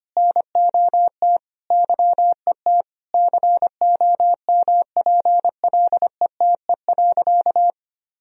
Morse code, heard as a keyed tone, about 25 wpm, 700 Hz. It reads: NOT YET COMPLETE.
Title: Sentences from Top 1000 Words - Review Part 2 - 25wpm